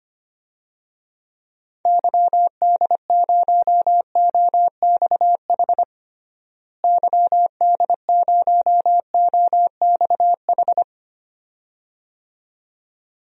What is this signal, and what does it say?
Morse code, heard as a keyed tone, about 25 wpm, 700 Hz. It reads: YD0OXH YD0OXH
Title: DXCC-100 Call Signs - 6 Characters - Repeated 2x - 25wpm